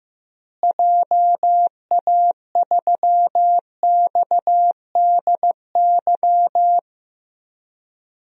Morse code, heard as a keyed tone, about 15 wpm, 700 Hz. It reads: JA3XDY
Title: DXCC-100 Call Signs - 6 Characters - Repeated 2x - 15wpm